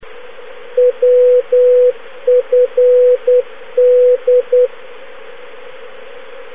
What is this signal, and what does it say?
Morse code, heard as a keyed tone, about 10 wpm, 500 Hz. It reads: WFD
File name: europe / g-709-wfd-380 woodford aerodrome, manchester, uk (now closed, alan gale)